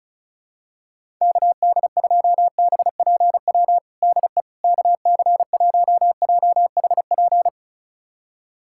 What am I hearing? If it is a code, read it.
Morse code, heard as a keyed tone, about 36 wpm, 700 Hz. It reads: KD2BPW DE KC1JHP
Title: USA Callsign DE Callsign 35wpm